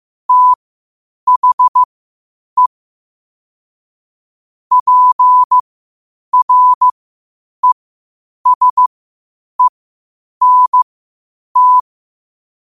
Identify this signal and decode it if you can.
Morse code, heard as a keyed tone, about 15 wpm, spaced as for 8 wpm, 1000 Hz. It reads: THE PRESENT
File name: qotdfiles / 8wpm